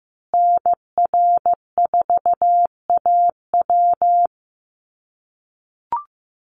Morse code, heard as a keyed tone, about 15 wpm, 700 Hz. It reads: NR4AW
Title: USA Call Signs Repeated 3x 15wpm